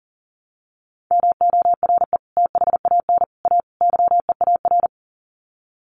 Morse code, heard as a keyed tone, about 40 wpm, 700 Hz. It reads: MORE THAN A YEAR
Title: Sets of 4 Words 40wpm